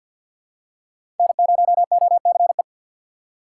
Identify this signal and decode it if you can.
Morse code, heard as a keyed tone, about 50 wpm, 700 Hz. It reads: N0OCE
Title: USA Call Signs - 6x Character Spacing - 50wpm